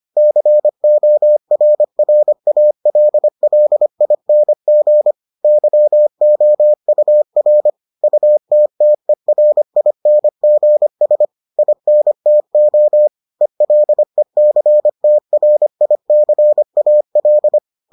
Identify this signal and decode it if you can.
Morse code, heard as a keyed tone, about 25 wpm, 600 Hz. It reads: CORRALLING YOUR UTTERINGS INTO ELECTRICAL